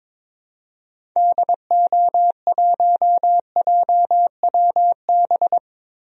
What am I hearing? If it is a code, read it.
Morse code, heard as a keyed tone, about 22 wpm, 700 Hz. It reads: DO1JWB
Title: DXCC-100 Call Signs - 6 Characters - 22wpm